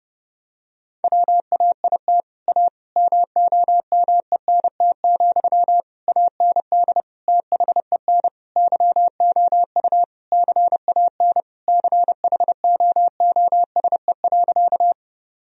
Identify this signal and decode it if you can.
Morse code, heard as a keyed tone, about 30 wpm, 700 Hz. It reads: WAIT A MOMENT, AND THEN YOU CAN CHOOSE.
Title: Sentences from Top 1000 Words - Review Part 4 - 30wpm